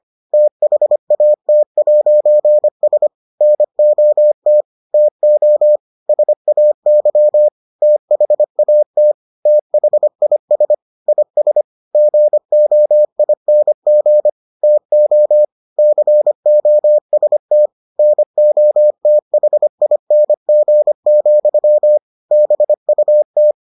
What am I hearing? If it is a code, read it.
Morse code, heard as a keyed tone, about 25 wpm, 600 Hz. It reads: THAT'S NOT TO SAY THAT THIS IS GOING TO COST NOTHING, BUT